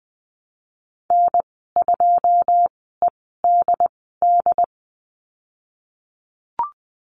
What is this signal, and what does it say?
Morse code, heard as a keyed tone, about 20 wpm, spaced as for 15 wpm, 700 Hz. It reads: N2EDD
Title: 128 Lesson 21g - Number 6 - Call Signs 20wpm